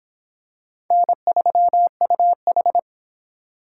Morse code, heard as a keyed tone, about 25 wpm, 700 Hz. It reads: N3UH